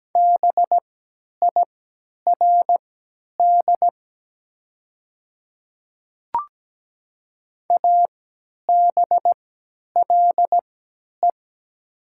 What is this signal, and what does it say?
Morse code, heard as a keyed tone, about 17 wpm, spaced as for 10 wpm, 700 Hz. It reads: BIRD ABLE
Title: 3x Character Spacing - Top 500 Words - 17wpm